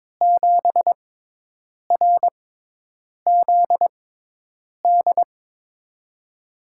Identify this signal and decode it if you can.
Morse code, heard as a keyed tone, about 22 wpm, spaced as for 8 wpm, 700 Hz. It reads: 7RZD
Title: USA Call Signs - 6x Character Spacing - 22wpm